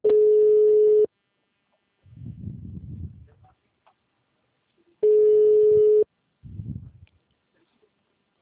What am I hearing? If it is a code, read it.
Ringback tone (Europe, 425 Hz).